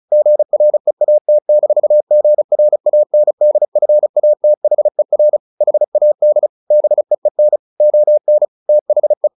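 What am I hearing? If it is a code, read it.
Morse code, heard as a keyed tone, about 36 wpm, 600 Hz. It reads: GREAT-GRANDFATHER HAD BEEN ON THE
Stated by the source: Morse practice at 35 wpm from CW med Gnister, sending Ken Follett's novel Never in English.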